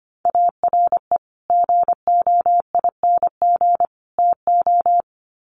Morse code, heard as a keyed tone, about 25 wpm, 700 Hz. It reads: ARE GOING TO
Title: Sets of 3 Words 25wpm